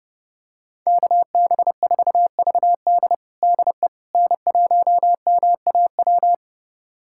Morse code, heard as a keyed tone, about 30 wpm, 700 Hz. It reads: KB4VD DE N1MAW
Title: USA Callsign DE Callsign - 5 Characters - 30wpm